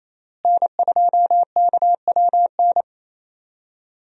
Morse code, heard as a keyed tone, about 28 wpm, 700 Hz. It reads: N2KWN
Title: USA Call Signs - 5 Characters - Repeated 3x - 28wpm